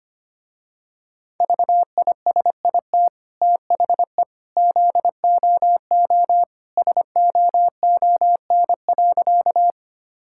Morse code, heard as a keyed tone, about 25 wpm, 700 Hz. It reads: VISIT THE ZOO SOON.
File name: Easy Sentences with Hints - 25wpm